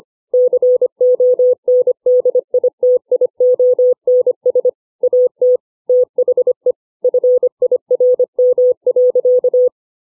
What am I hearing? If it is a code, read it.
Morse code, heard as a keyed tone, about 25 wpm, 500 Hz. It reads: CONDITIONS AT THE FIRM.